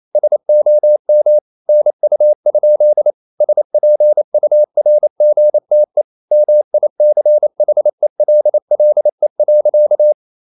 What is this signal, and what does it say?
Morse code, heard as a keyed tone, about 28 wpm, 600 Hz. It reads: SOM NU? SPURGTE MICHELLE.
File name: Ferie QRQ 28wpm